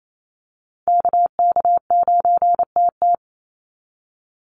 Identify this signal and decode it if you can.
Morse code, heard as a keyed tone, about 28 wpm, 700 Hz. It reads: KK9TT
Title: USA Call Signs 28wpm